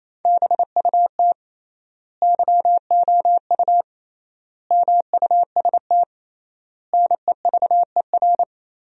Morse code, heard as a keyed tone, about 28 wpm, 700 Hz. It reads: BUT YOU MUST NEVER
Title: 3x Word-Spacing - Sentences from Top 300 Words - 28wpm